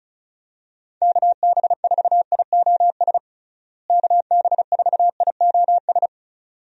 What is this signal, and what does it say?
Morse code, heard as a keyed tone, about 36 wpm, 700 Hz. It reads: KB4IOS KB4IOS
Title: USA Call Signs - 6 Characters - Repeated 2x - 35wpm